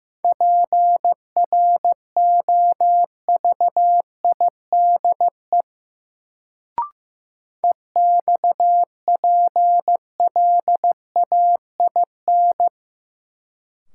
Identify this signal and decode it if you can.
Morse code, heard as a keyed tone, about 15 wpm, 700 Hz. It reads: PROVIDE EXPLAIN
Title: Sentences from Top 1100 Words - New Word Drill Alt - 15wpm